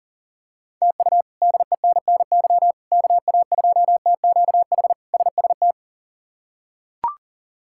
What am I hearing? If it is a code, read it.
Morse code, heard as a keyed tone, about 37 wpm, 700 Hz. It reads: TU DENNY KA2TQH SST
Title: K1USN SST QSO Practice - 40wpm